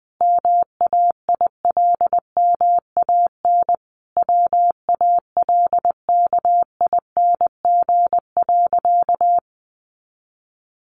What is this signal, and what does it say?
Morse code, heard as a keyed tone, about 20 wpm, 700 Hz. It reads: MAILMAN WALKING.
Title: Easy Phrases 20wpm